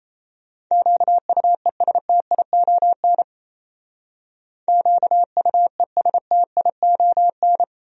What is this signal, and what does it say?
Morse code, heard as a keyed tone, about 34 wpm, 700 Hz. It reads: QUESTION QUESTION
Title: Speed-Racing Top 500 Words 28wpm